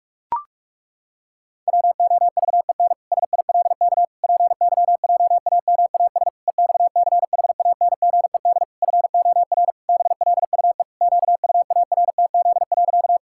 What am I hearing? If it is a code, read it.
Morse code, heard as a keyed tone, about 46 wpm, 700 Hz. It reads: WOVEN SILK PYJAMAS EXCHANGED FOR BLUE QUARTZ.
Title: Pangram Challenge v2 - 45wpm